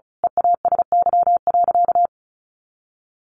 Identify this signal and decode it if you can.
Morse code, heard as a keyed tone, about 34 wpm, 700 Hz. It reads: EASY.
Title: Sentences from Top 700 Words 35wpm